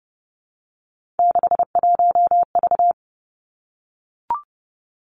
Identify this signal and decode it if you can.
Morse code, heard as a keyed tone, about 30 wpm, 700 Hz. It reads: 61V
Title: Three Challenging Letter-Number Combinations 30wpm